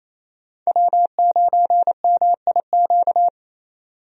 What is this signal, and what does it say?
Morse code, heard as a keyed tone, about 28 wpm, 700 Hz. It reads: W9MIQ
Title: USA Call Signs - 5 Characters - 28wpm